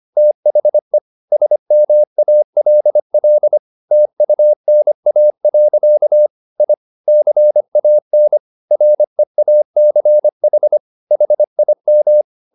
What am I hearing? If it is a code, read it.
Morse code, heard as a keyed tone, about 25 wpm, 600 Hz. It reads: THE SMALL TUNA. I CAN REACH HIM
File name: the old man and the sea part 1 25wpm